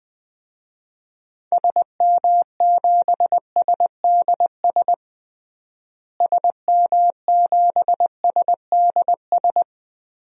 Morse code, heard as a keyed tone, about 20 wpm, 700 Hz. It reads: SM7SDS SM7SDS
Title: DXCC-100 Call Signs - 6 Characters - Repeated 3x - 20wpm